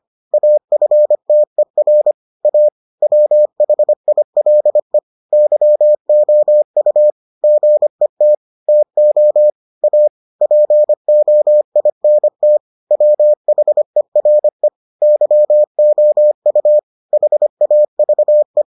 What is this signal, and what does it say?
Morse code, heard as a keyed tone, about 25 wpm, 600 Hz. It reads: AFTER A WHILE YOU GET TO A POINT WHERE YOU HAVE